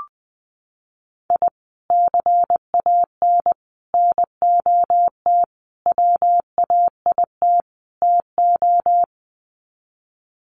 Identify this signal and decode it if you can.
Morse code, heard as a keyed tone, about 20 wpm, 700 Hz. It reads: I CAN NOT WAIT TO
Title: Sets of 5 Words Encore 20wpm